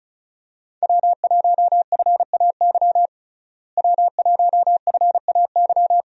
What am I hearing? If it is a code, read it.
Morse code, heard as a keyed tone, about 36 wpm, 700 Hz. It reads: W1FAY W1FAY
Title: USA Call Signs - 5 Characters - Repeated 2x - 35wpm